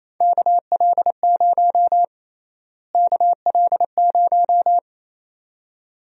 Morse code, heard as a keyed tone, about 28 wpm, 700 Hz. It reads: KL0 KL0
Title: USA Call Sign Prefixes - Repeated 2x - 28wpm